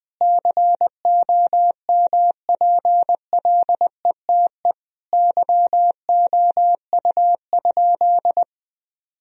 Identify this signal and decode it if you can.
Morse code, heard as a keyed tone, about 21 wpm, 700 Hz. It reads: COMPLETE YOU?
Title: Sentences from Top 1000 Words - Review Part 2 - 20wpm